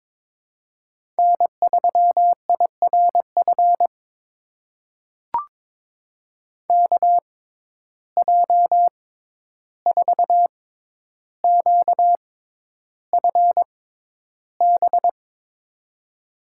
Morse code, heard as a keyed tone, about 22 wpm, spaced as for 11 wpm, 700 Hz. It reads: N3IRF KJ4QFB
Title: USA Call Signs - 6x Character Spacing - 22wpm